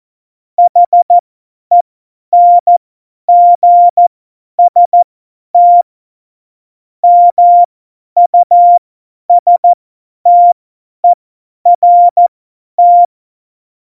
Morse code, heard as a keyed tone, about 14 wpm, spaced as for 10 wpm, 700 Hz. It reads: HENGST MUSTERT
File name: Morsekurs / Witze / Witz 15. Dezember 2020, 50 F70